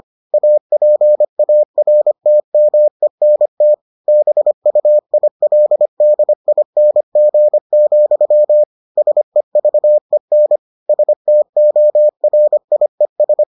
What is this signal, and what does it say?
Morse code, heard as a keyed tone, about 25 wpm, 600 Hz. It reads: APARTMENT BUILDING, SEVEN STORIES